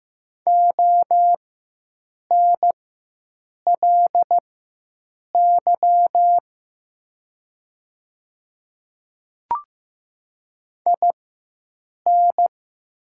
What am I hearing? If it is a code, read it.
Morse code, heard as a keyed tone, about 15 wpm, spaced as for 7 wpm, 700 Hz. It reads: ONLY IN
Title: 4x Character Spacing - Top 300 Words - 15wpm